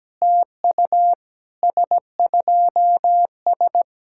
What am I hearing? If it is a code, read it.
Morse code, heard as a keyed tone, about 17 wpm, 700 Hz. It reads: TU S2S
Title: SOTA QSO Practice 17wpm